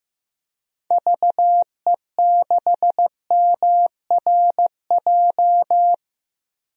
Morse code, heard as a keyed tone, about 16 wpm, 700 Hz. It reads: VE6MRJ
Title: DXCC-100 Call Signs - 6 Characters - Repeated 2x - 15wpm